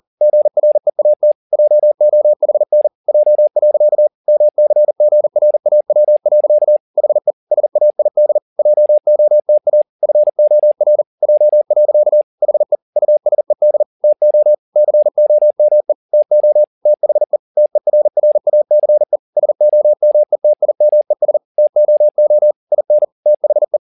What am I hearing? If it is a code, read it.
Morse code, heard as a keyed tone, about 40 wpm, 600 Hz. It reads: GREAT JOHN J. MCGRAW. HE SAID JOTA FOR J. HE USED TO COME TO THE TERRACE SOMETIMES TOO IN THE